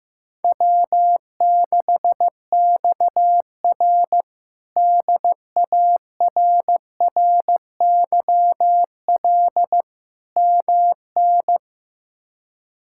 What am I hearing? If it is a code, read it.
Morse code, heard as a keyed tone, about 15 wpm, 700 Hz. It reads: W6XR DARRYL MN
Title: K1USN SST QSO Practice - 15wpm